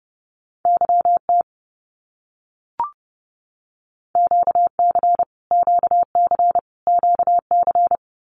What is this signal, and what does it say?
Morse code, heard as a keyed tone, about 30 wpm, 700 Hz. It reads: YT QC QC QC
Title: Canadian Provinces and Territories - Repeated 3x - 30wpm